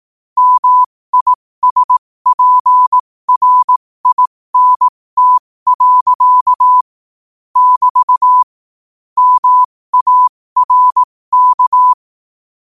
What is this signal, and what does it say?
Morse code, heard as a keyed tone, about 18 wpm, spaced as for 15 wpm, 1000 Hz. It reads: MISPRINT. <BT> MARK
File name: qotdfiles / 15wpm